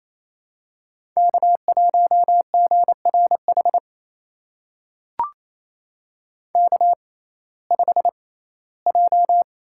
Morse code, heard as a keyed tone, about 27 wpm, spaced as for 17 wpm, 700 Hz. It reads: K1GRH K5J
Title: USA Call Signs - 6x Character Spacing - 28wpm